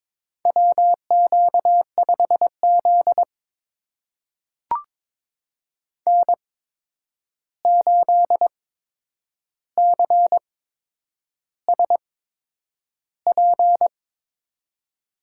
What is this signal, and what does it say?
Morse code, heard as a keyed tone, about 22 wpm, spaced as for 9 wpm, 700 Hz. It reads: WQ5Z N8CSP
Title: USA Call Signs - 8x Character Spacing - 22wpm